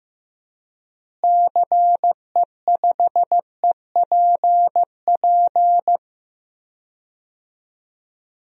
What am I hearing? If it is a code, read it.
Morse code, heard as a keyed tone, about 15 wpm, 700 Hz. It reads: CE5EPP
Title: DXCC-100 Call Signs - 6 Characters - Repeated 3x - 15wpm